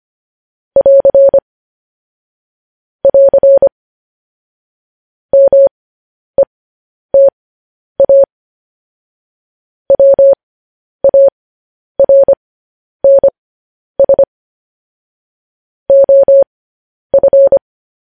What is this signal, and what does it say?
Morse code, heard as a keyed tone, about 25 wpm, spaced as for 10 wpm, 550 Hz. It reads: <AR> <AR> META WARNS OF
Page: News Headlines in Morse Code at 10 WPM